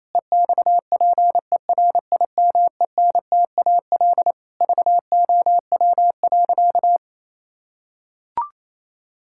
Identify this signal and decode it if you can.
Morse code, heard as a keyed tone, about 28 wpm, 700 Hz. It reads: EXPERIMENTAL VOW.
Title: Pangram Challenge v2 - 28wpm